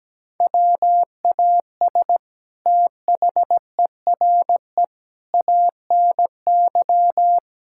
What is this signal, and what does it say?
Morse code, heard as a keyed tone, about 17 wpm, 700 Hz. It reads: WAS THERE ANY
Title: Sentences from Top 1600 Words 17wpm